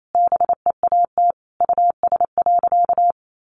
Morse code, heard as a keyed tone, about 28 wpm, 700 Hz. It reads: BEAT US.
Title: Sentences from Top 1300 Words 28wpm